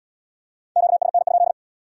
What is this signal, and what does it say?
Morse code, heard as a keyed tone, about 74 wpm, 700 Hz. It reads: QSN?